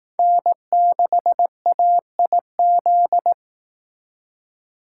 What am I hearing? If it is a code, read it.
Morse code, heard as a keyed tone, about 18 wpm, 700 Hz. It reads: N6AIZ
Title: Speed Racing - USA 1x3 Call Signs 15wpm